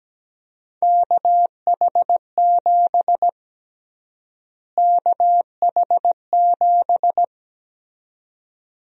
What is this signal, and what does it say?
Morse code, heard as a keyed tone, about 17 wpm, 700 Hz. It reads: KH7 KH7
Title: USA Call Sign Prefixes - Repeated 2x - 17wpm